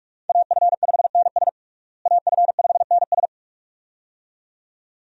Morse code, heard as a keyed tone, about 41 wpm, 700 Hz. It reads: AF5NS AF5NS
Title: USA Call Signs - 5 Characters - Repeated 2x - 45wpm